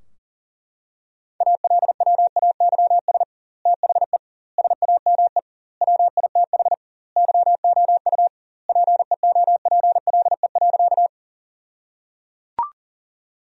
Morse code, heard as a keyed tone, about 37 wpm, 700 Hz. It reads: ALWAYS THE SAME WITH YOU PEOPLE.
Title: 2x Word-Spacing - Sentences from Top 300 Words - 40wpm